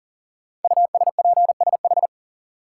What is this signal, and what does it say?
Morse code, heard as a keyed tone, about 43 wpm, 700 Hz. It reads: USPSH
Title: Code Groups - 5 Letters - 40wpm